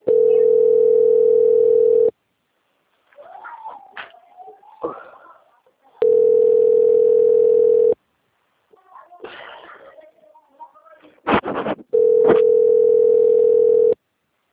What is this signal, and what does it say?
Ringback tone (North America), 440 + 480 Hz, with speech.
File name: out- - - - - .7645